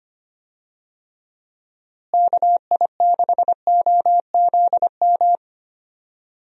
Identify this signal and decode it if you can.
Morse code, heard as a keyed tone, about 25 wpm, 700 Hz. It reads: KI6OZM
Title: USA Call Signs - 6 Characters - Repeated 3x - 25wpm